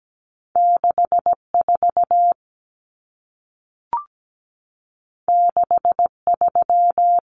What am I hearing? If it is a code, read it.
Morse code, heard as a keyed tone, about 17 wpm, 700 Hz. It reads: 64 63
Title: Numbers - 2-Digit 17wpm